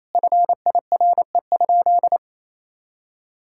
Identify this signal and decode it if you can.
Morse code, heard as a keyed tone, about 28 wpm, 690 Hz. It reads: FIRE?
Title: Sentences from Top 1000 Words - Review Part 6 - 28wpm